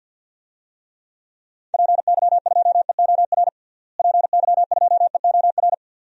Morse code, heard as a keyed tone, about 48 wpm, 700 Hz. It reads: PY2EOR PY2EOR
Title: DXCC-100 Call Signs - 6 Characters - Repeated 2x - 50wpm